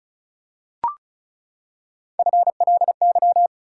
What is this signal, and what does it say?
Morse code, heard as a keyed tone, about 36 wpm, 700 Hz. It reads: FLY
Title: Three Letter Words - 35wpm